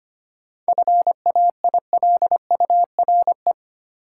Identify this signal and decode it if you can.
Morse code, heard as a keyed tone, about 25 wpm, 700 Hz. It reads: FAILURE
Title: Sentences from Top 900 Words - New Word Drill - 25wpm